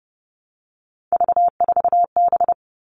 Morse code, heard as a keyed tone, about 30 wpm, 700 Hz. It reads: V4B
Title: All Cut Numbers - 3-Digit 30wpm